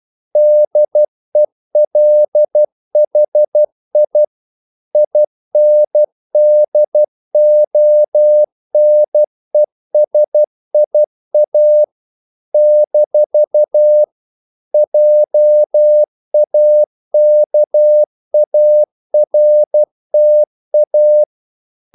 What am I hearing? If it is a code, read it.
Morse code, heard as a keyed tone, about 12 wpm, 600 Hz. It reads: DELHI INDONESIA - JAKARTA
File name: CW / Practice / NationalCapitals12